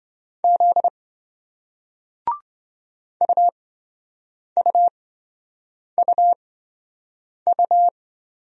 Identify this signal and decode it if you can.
Morse code, heard as a keyed tone, about 27 wpm, spaced as for 7 wpm, 700 Hz. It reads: Z UUUU